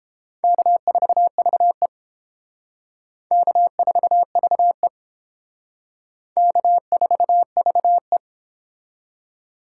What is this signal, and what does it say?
Morse code, heard as a keyed tone, about 30 wpm, 700 Hz. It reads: K4VE K4VE K4VE